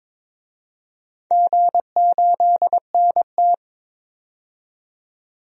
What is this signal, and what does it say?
Morse code, heard as a keyed tone, about 22 wpm, 700 Hz. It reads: G8NT
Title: DXCC-100 Call Signs - 4 Characters - 22wpm